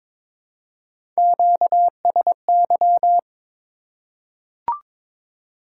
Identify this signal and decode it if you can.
Morse code, heard as a keyed tone, about 22 wpm, 700 Hz. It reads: QSY
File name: Top 90 QSO Elements - 3x Character Spacing - 22wpm